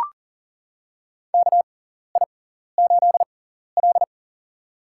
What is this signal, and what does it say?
Morse code, heard as a keyed tone, about 40 wpm, spaced as for 14 wpm, 700 Hz. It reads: KI8L
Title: USA Call Signs - 4 Characters - 6x Character Spacing - 40wpm